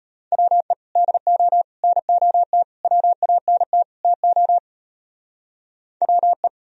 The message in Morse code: WE DO NOT WANT TO WE

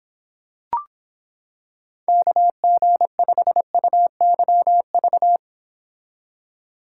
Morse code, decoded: KG5UYV

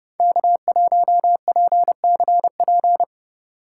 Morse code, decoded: K1PCP

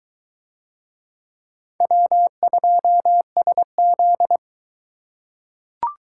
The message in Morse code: W2SZ

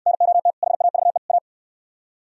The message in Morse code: NOT HAVE A

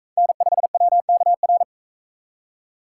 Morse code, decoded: N5WKR

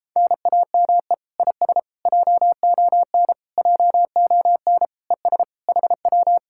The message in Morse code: NAME IS JON JON ES HW

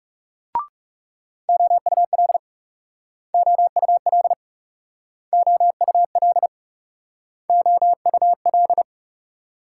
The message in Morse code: OUL OUL OUL OUL

